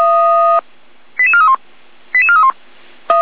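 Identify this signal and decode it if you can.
Five-tone selective call (ZVEI), heard as quick runs of five tones, tones of about 70 ms. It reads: 88431 88431
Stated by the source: ZVEI five-tone call received on an SDR.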